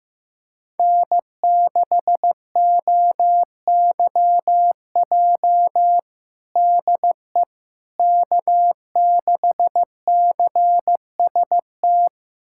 Morse code, heard as a keyed tone, about 15 wpm, 700 Hz. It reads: N6OYJ DE K6CST
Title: USA Callsign DE Callsign - 5 Characters - 15wpm